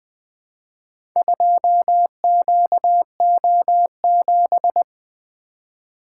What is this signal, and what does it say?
Morse code, heard as a keyed tone, about 20 wpm, 700 Hz. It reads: 2QO7
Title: Four Letter-Number Combinations - 20wpm